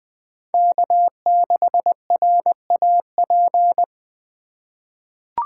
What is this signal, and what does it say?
Morse code, heard as a keyed tone, about 21 wpm, 700 Hz. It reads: K6RAP
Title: USA Call Signs - 5 Characters - 4x Character Spacing - 20wpm